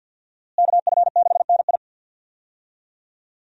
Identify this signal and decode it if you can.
Morse code, heard as a keyed tone, about 51 wpm, 700 Hz. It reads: KF6NI
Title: USA Call Signs - 5 Characters - Repeated 3x - 50wpm